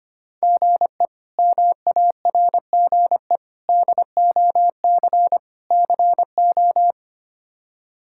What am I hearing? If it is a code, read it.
Morse code, heard as a keyed tone, about 25 wpm, 700 Hz. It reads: GE MARGE DOC CO
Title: K1USN SST QSO Practice - 25wpm